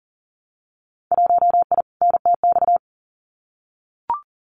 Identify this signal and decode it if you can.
Morse code, heard as a keyed tone, about 40 wpm, 700 Hz. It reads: 1I NTX
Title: Winter Field Day Exchanges 40wpm